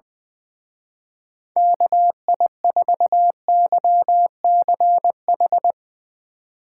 Morse code, heard as a keyed tone, about 20 wpm, 700 Hz. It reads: KI4YCH
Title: Speed Racing - USA 2x3 Call Signs 20wpm